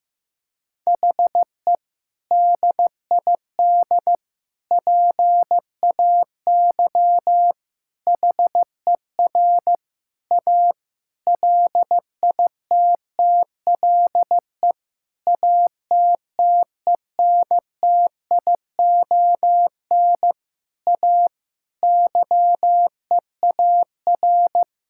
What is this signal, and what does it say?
Morse code, heard as a keyed tone, about 16 wpm, 700 Hz. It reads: HE DID PAY HER A LITTLE ATTENTION A YEAR
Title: Sentences from Top 1000 Words - Review Part 6 - 15wpm